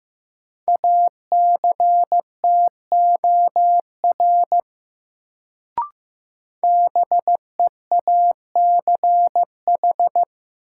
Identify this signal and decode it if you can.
Morse code, heard as a keyed tone, about 15 wpm, 700 Hz. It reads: ACTOR BEACH
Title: Sentences from Top 1500 Words - New Word Drill - 15wpm